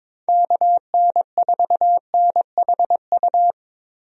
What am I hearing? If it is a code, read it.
Morse code, heard as a keyed tone, about 22 wpm, 700 Hz. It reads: KN4NHU
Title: USA Call Signs - 6 Characters - Repeated 2x - 22wpm